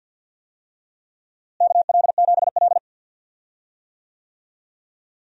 Morse code, heard as a keyed tone, about 53 wpm, 700 Hz. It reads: KL7L
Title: DXCC-100 Call Signs - 4 Characters - Repeated 3x - 50wpm